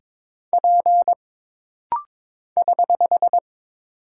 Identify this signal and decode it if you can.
Morse code, heard as a keyed tone, about 22 wpm, 700 Hz. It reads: P <HH>
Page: This rapid-fire timing is geared to push your ICR (Instant Character Recognition) skills.